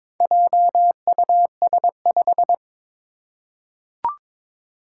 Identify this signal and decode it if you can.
Morse code, heard as a keyed tone, about 22 wpm, 700 Hz. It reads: JUS5
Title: Four Challenging Letter-Number Combinations 22wpm